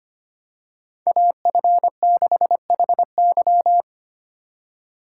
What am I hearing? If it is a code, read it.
Morse code, heard as a keyed tone, about 25 wpm, 700 Hz. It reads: AF6HY
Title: USA Call Signs - 5 Characters - 4x Character Spacing - 25wpm